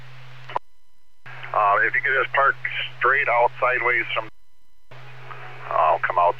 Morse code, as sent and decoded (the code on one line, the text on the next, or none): none